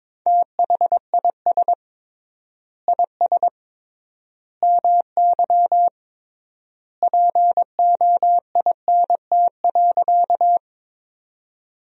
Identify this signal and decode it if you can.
Morse code, heard as a keyed tone, about 22 wpm, 700 Hz. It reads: THIS IS MY POINT.
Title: 3x Word-Spacing - Sentences from Top 200 Words - 22wpm